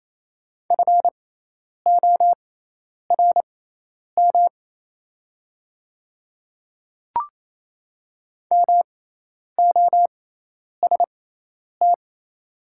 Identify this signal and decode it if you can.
Morse code, heard as a keyed tone, about 28 wpm, spaced as for 10 wpm, 700 Hz. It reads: FORM MOST
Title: 6x Character Spacing - Top 200 Words - 28wpm